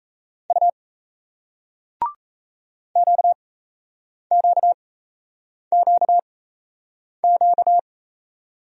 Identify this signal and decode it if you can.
Morse code, heard as a keyed tone, about 38 wpm, spaced as for 8 wpm, 700 Hz. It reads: U QQQQ